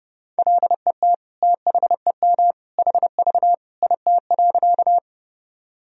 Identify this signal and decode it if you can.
Morse code, heard as a keyed tone, about 30 wpm, 700 Hz. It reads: LET THEM HV IT.
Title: Sentences from Top 300 Words and Abbreviations 30wpm